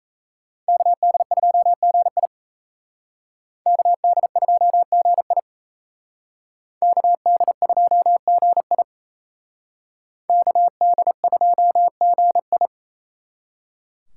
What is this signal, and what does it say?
Morse code, heard as a keyed tone, about 35 wpm, 700 Hz. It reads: KD2GI KD2GI KD2GI KD2GI
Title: Speed Racing - USA 2x2 Call Signs 28wpm